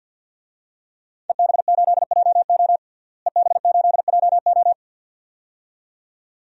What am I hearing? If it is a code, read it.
Morse code, heard as a keyed tone, about 53 wpm, 700 Hz. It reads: EB8JO EB8JO
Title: DXCC-100 Call Signs - 5 Characters - Repeated 2x - 50wpm